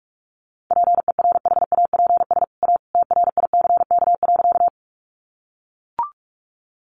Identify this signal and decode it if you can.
Morse code, heard as a keyed tone, about 44 wpm, 700 Hz. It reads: PERHAPS A TRICK.
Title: Sentences from Top 1500 Words 45wpm